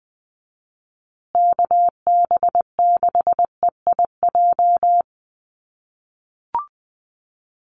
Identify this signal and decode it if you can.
Morse code, heard as a keyed tone, about 20 wpm, 700 Hz. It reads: KB6EIJ
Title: USA Call Signs Repeated 2x 20wpm